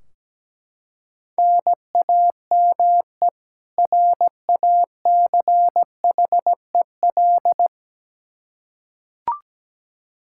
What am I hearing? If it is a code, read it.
Morse code, heard as a keyed tone, about 17 wpm, 700 Hz. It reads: NAME RACHEL
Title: Practice QSO - Exchange 1 - Beginning - 17wpm